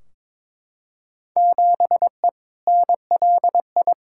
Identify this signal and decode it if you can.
Morse code, heard as a keyed tone, about 22 wpm, 700 Hz. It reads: 7E NLI
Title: Field Day Exchanges Encore - 22wpm